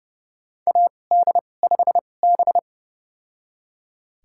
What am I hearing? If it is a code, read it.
Morse code, heard as a keyed tone, about 30 wpm, spaced as for 22 wpm, 700 Hz. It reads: AD5B
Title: USA Call Signs - 4 Characters - 2x Character Spacing - 30wpm